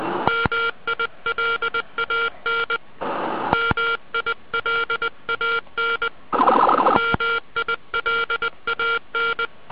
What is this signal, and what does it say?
Morse code, heard as a keyed tone, about 20 wpm, 1390 Hz, with noise. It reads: MILAN MILAN MILAN